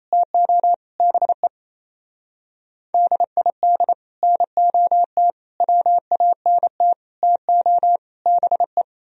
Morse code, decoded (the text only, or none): TO BE DID NOT WANT TO BE